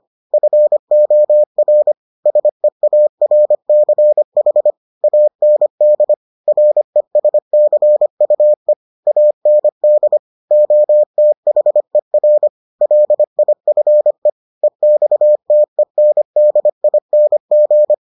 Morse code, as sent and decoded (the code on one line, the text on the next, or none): ..-. --- .-. / ... .- .-. -.-. .... / .- -. -.. / .-. ... -.-. ..- . / .- -. -.. / --- - .... .-. / .-.. .. ..-. / . -..- - . -. -.. .. -. --.
FOR SEARCH AND RESCUE AND OTHER LIFE EXTENDING